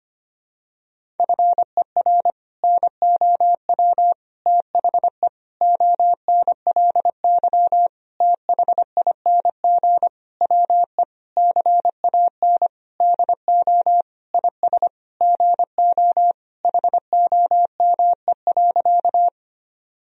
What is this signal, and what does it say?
Morse code, heard as a keyed tone, about 25 wpm, 700 Hz. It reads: FER NOW THE ONLY THING WE CAN DO IS GO HOME.
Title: Sentences from Top 200 Words and Abbreviations 25wpm